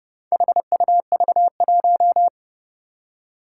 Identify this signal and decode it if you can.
Morse code, heard as a keyed tone, about 30 wpm, 700 Hz. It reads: HUV1
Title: Four Challenging Letter-Number Combinations 30wpm